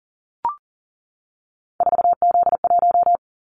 Morse code, heard as a keyed tone, about 38 wpm, 700 Hz. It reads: 4Z1
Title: Callsign Prefixes for Top 50 DXCC - 40wpm